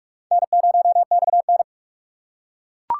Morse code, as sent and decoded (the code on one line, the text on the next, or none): -. ----- -..- -.
N0XN